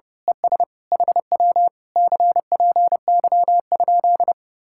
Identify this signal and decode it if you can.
Morse code, heard as a keyed tone, about 24 wpm, 700 Hz. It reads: ES HW CPY?
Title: Speed-Racing Common QSO Phrases - 25wpm